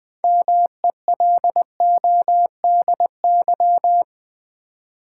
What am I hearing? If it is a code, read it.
Morse code, heard as a keyed tone, about 20 wpm, 700 Hz. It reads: MELODY
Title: Build A Word - 3 Letters At A Time - 6-Letter Words - 20wpm